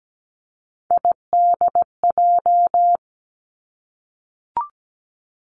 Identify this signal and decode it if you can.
Morse code, heard as a keyed tone, about 17 wpm, 700 Hz. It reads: IDJ
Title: Three Challenging Letter-Number Combinations 17wpm